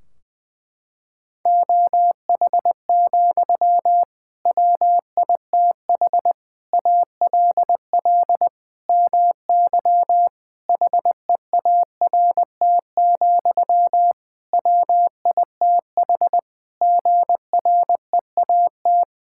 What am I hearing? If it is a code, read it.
Morse code, heard as a keyed tone, about 20 wpm, 700 Hz. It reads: OH, WITH ALL MY HEART, WITH GREAT
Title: Sentences from Top 1600 Words 20wpm